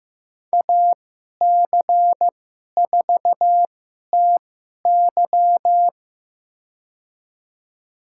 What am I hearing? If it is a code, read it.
Morse code, heard as a keyed tone, about 16 wpm, spaced as for 11 wpm, 700 Hz. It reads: AC4TY
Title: USA Call Signs - 5 Characters - 2x Character Spacing - 15wpm